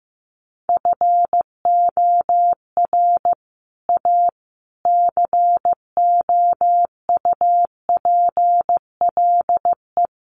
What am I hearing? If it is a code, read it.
Morse code, heard as a keyed tone, about 15 wpm, 700 Hz. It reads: FOR A COUPLE
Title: Sets of 5 Words Encore2 15wpm